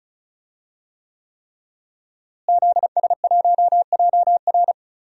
Morse code, heard as a keyed tone, about 36 wpm, 700 Hz. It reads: ZS1JR
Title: DXCC-100 Call Signs - 5 Characters - Repeated 3x - 35wpm